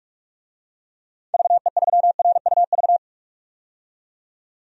Morse code, heard as a keyed tone, about 46 wpm, 700 Hz. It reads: VE3RUV
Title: DXCC-100 Call Signs - 6 Characters - 45wpm